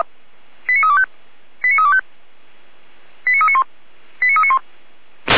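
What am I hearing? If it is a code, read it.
Five-tone selective call (ZVEI), heard as quick runs of five tones, tones of about 70 ms. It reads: 89325 89325 89391 89391
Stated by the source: ZVEI five-tone selective call received on an SDR.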